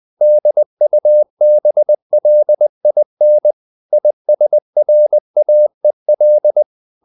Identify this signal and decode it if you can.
Morse code, heard as a keyed tone, about 20 wpm, 600 Hz. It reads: DUBLIN ISRAEL
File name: CW / Practice / NationalCapitals20